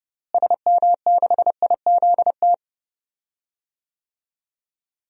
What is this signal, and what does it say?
Morse code, heard as a keyed tone, about 29 wpm, 700 Hz. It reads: SM6IZT
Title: DXCC-100 Call Signs - 6 Characters - Repeated 3x - 30wpm